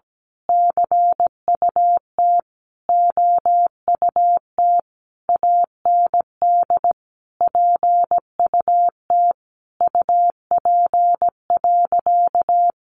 Morse code, heard as a keyed tone, about 17 wpm, 700 Hz. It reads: CUT OUT AND PUT UP.
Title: A Sound Of Thunder 17wpm